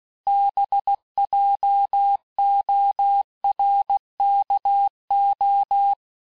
Morse code, heard as a keyed tone, about 16 wpm, 800 Hz. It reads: BJORKO
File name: texte-clair-vitesse-15mpm-puis-20mpm